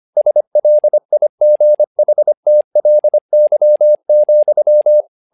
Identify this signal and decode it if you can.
Morse code, heard as a keyed tone, about 25 wpm, 600 Hz. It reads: SLIGHTLY,
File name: learning center / tower trouble / tower trouble 25wpm